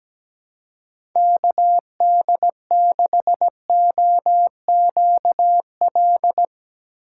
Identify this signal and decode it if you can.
Morse code, heard as a keyed tone, about 17 wpm, 700 Hz. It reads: KD6OQL
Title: USA Call Signs - 6 Characters - 4x Character Spacing - 17wpm